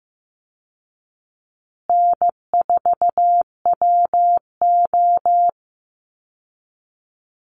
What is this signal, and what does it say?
Morse code, heard as a keyed tone, about 15 wpm, 700 Hz. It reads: N4WO
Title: Call Signs Repeated 2x - 15wpm